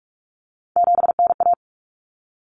Morse code, heard as a keyed tone, about 44 wpm, 700 Hz. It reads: 7NA
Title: Common Cut Numbers - 3-Digit 45wpm